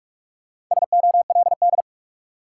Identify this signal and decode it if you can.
Morse code, heard as a keyed tone, about 46 wpm, 700 Hz. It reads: SOLD
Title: Sentences from Top 1400 Words - New Word Drill Alt - 45wpm